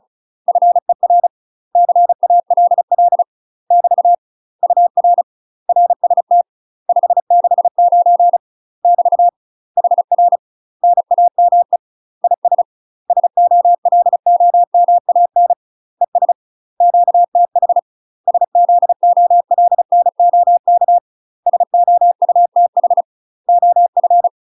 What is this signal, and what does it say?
Morse code, heard as a keyed tone, about 36 wpm, 700 Hz. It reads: FER CALL <BT> UR RST 569 <BT> HR NAME IS SOLOMAN ES QTH SZOLNOK SOUTH OF
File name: UK MASTER QSOs TRAINING/35WPM